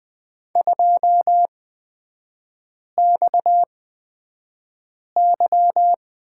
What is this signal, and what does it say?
Morse code, heard as a keyed tone, about 20 wpm, spaced as for 5 wpm, 700 Hz. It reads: 2XY
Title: 229 Lesson 37e - Letter x - Three Characters 20wpm